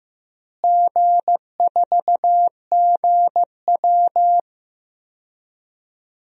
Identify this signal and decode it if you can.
Morse code, heard as a keyed tone, about 16 wpm, 700 Hz. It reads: G4GW
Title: DXCC-100 Call Signs - 4 Characters - Repeated 3x - 15wpm